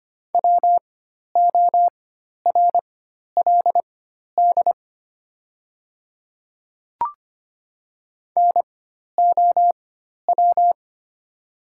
Morse code, heard as a keyed tone, about 25 wpm, spaced as for 12 wpm, 700 Hz. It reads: WORLD NOW
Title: 4x Character Spacing - Top 300 Words - 25wpm